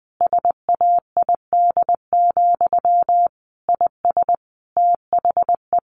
Morse code, heard as a keyed tone, about 20 wpm, 700 Hz. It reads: SAID, IS THE
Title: A Sound Of Thunder 20wpm